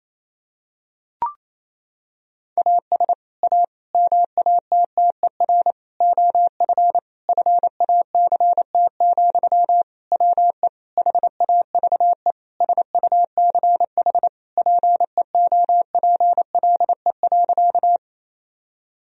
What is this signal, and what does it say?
Morse code, heard as a keyed tone, about 28 wpm, 700 Hz. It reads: AS A MATTER OF FACT, WE HAVE SUCH PEOPLE.
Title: Sentences from Top 1000 Words - Review Part 3 - 28wpm